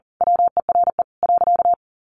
Morse code, heard as a keyed tone, about 40 wpm, 700 Hz. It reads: WERE.
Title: A Sound Of Thunder 40wpm